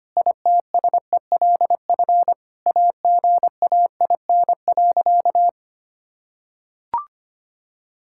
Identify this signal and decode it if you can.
Morse code, heard as a keyed tone, about 25 wpm, 700 Hz. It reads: ITSELF AGAIN.
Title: Sentences from Top 1000 Words - Review Part 5 - 25wpm